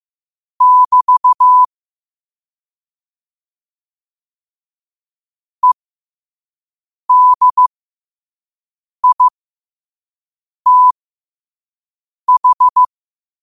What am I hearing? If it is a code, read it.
Morse code, heard as a keyed tone, about 15 wpm, spaced as for 5 wpm, 1000 Hz. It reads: <BT> EDITH
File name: qotdfiles / 5wpm